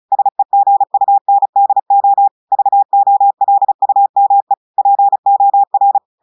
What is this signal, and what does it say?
Morse code, heard as a keyed tone, about 36 wpm, 800 Hz. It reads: SEGUNDO VOLUME POR
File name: audios / portugais / portugais texte en clair 1 vitesse 35 mots minutes